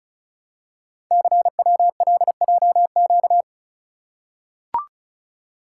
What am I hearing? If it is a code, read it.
Morse code, heard as a keyed tone, about 33 wpm, 700 Hz. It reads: CWLJQ
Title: Code Groups - 5 Letters - 35wpm